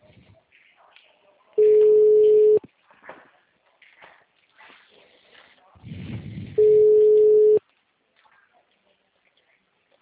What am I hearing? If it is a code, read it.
Ringback tone (Europe, 425 Hz).
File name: out- -2001- - - .14731